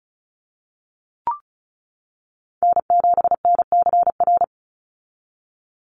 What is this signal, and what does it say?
Morse code, heard as a keyed tone, about 34 wpm, 700 Hz. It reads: N7NCR
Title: USA Call Signs 35wpm